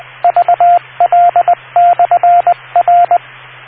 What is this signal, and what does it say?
Morse code, heard as a keyed tone, about 20 wpm, 700 Hz, with radio noise. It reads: VL/R